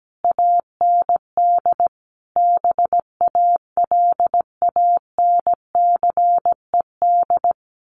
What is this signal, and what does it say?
Morse code, heard as a keyed tone, about 17 wpm, 700 Hz. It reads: AND BALANCED
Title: A Sound Of Thunder 17wpm